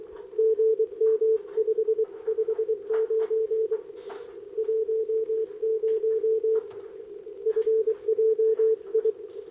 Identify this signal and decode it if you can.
Morse code, heard as a keyed tone, about 23 wpm, 440 Hz, with radio noise. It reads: GM 559 10 FJI
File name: OK QRP 2026 / OKQRP2026